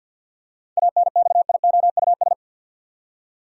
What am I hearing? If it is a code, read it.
Morse code, heard as a keyed tone, about 50 wpm, 700 Hz. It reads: ANXIOUS
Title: Seven Letter Words - 50wpm